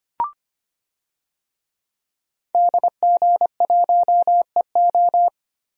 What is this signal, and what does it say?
Morse code, heard as a keyed tone, about 25 wpm, 700 Hz. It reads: DG1EO
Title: DXCC-100 Call Signs - 5 Characters - Repeated 3x - 25wpm